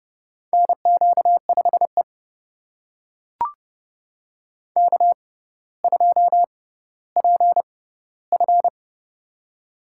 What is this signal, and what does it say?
Morse code, heard as a keyed tone, about 30 wpm, spaced as for 15 wpm, 700 Hz. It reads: NQ5E K2PF
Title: USA Call Signs - 4 Characters - 6x Character Spacing - 30wpm